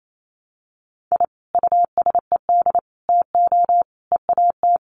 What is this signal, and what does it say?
Morse code, heard as a keyed tone, about 28 wpm, 700 Hz. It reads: I USED TO EAT
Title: Sentences from Top 1200 Words 28wpm